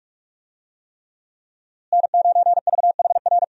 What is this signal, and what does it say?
Morse code, heard as a keyed tone, about 46 wpm, 700 Hz. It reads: N9VHR
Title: USA Call Signs - 5 Characters - Repeated 3x - 45wpm